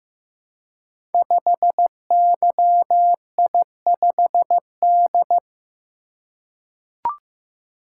Morse code, heard as a keyed tone, about 15 wpm, 700 Hz. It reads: 5YI5D
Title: Code Groups - 5 Letter:Numbers - 15wpm